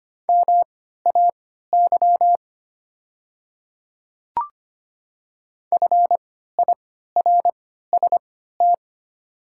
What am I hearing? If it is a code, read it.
Morse code, heard as a keyed tone, about 25 wpm, spaced as for 14 wpm, 690 Hz. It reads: MAY FIRST
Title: 3x Character Spacing - Top 100 Words - 25wpm